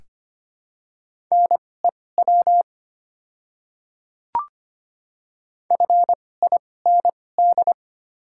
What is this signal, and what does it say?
Morse code, heard as a keyed tone, about 25 wpm, spaced as for 18 wpm, 700 Hz. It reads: NEW FIND
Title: 2x Character Spacing - Top 300 Words - 25wpm